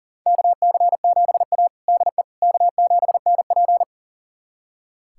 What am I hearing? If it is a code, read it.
Morse code, heard as a keyed tone, about 40 wpm, 700 Hz. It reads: KC7A DE K7NP
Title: USA Callsign DE Callsign - 4 Characters - 40wpm